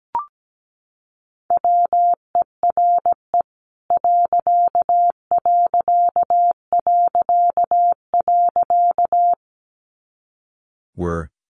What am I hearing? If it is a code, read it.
Morse code, heard as a keyed tone, about 17 wpm, 700 Hz. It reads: WERE ....
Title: A Sound Of Thunder 17wpm